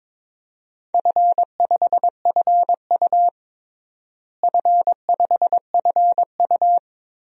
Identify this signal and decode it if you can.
Morse code, heard as a keyed tone, about 22 wpm, 700 Hz. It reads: F5FU F5FU
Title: DXCC-100 Call Signs - 4 Characters - Repeated 2x - 22wpm